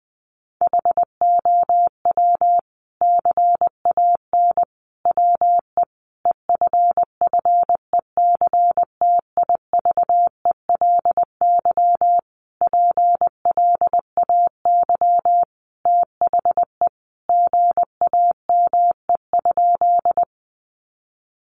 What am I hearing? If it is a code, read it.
Morse code, heard as a keyed tone, about 20 wpm, 700 Hz. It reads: HOW CAN WE EFFECTIVELY PLAY THE GAME?
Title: Sentences from Top 1400 Words 20wpm